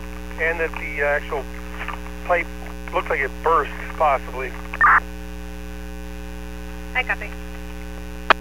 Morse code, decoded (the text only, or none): none